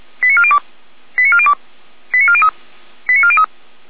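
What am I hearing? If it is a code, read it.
Five-tone selective call (ZVEI), heard as quick runs of five tones, tones of about 80 ms. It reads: 89402 89402 89403 89403
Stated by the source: ZVEI five-tone call 89711 received by SDR, Feuerwehr Orlamünde.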